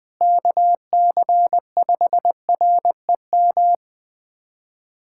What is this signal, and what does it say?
Morse code, heard as a keyed tone, about 20 wpm, 700 Hz. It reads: KC5REM
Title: USA Callsign DE Callsign 20wpm